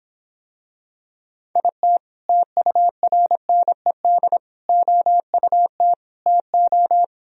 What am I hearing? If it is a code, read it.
Morse code, heard as a keyed tone, about 26 wpm, 700 Hz. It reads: IT TURNED OUT TO